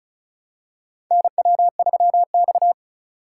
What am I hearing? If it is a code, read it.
Morse code, heard as a keyed tone, about 36 wpm, 700 Hz. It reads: NW3X
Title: USA Call Signs - 3x Character Spacing - 35wpm